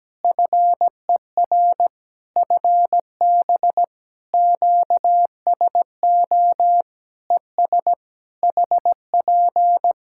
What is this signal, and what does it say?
Morse code, heard as a keyed tone, about 17 wpm, 700 Hz. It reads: FER FB QSO ES HP
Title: Practice QSO - Exchange 4 - Ending - 17wpm